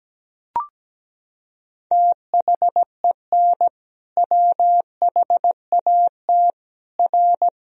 Morse code, heard as a keyed tone, about 17 wpm, 700 Hz. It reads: THEN WHAT R
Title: Sentences from Top 100 Words and Abbreviations 17wpm